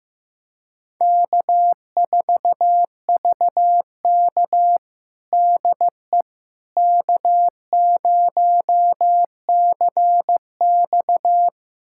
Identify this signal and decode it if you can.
Morse code, heard as a keyed tone, about 16 wpm, 700 Hz. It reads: K4VK DE K0CX
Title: USA Callsign DE Callsign - 4 Characters - 15wpm